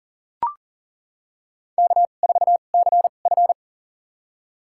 Morse code, heard as a keyed tone, about 37 wpm, spaced as for 28 wpm, 700 Hz. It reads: K4CF
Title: USA Call Signs - 4 Characters - 2x Character Spacing - 40wpm